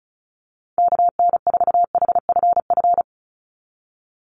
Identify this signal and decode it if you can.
Morse code, heard as a keyed tone, about 36 wpm, 700 Hz. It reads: KN4HFF
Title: USA Call Signs 35wpm